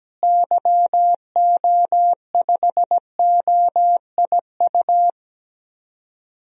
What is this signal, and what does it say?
Morse code, heard as a keyed tone, about 17 wpm, 700 Hz. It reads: YO5OIU
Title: DXCC-100 Call Signs - 6 Characters - Repeated 3x - 17wpm